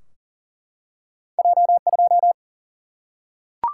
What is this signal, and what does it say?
Morse code, heard as a keyed tone, about 38 wpm, 700 Hz. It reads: J2
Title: Single-Letter Single-Number 40wpm